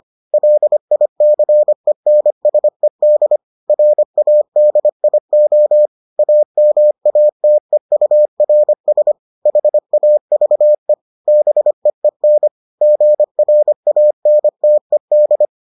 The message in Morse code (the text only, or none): LICENSED RADIO AMATEURS HAVE BEEN GRANTED